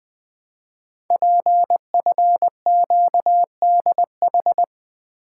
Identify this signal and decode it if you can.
Morse code, heard as a keyed tone, about 20 wpm, 700 Hz. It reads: PFQDH
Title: Code Groups - 5 Letters - 20wpm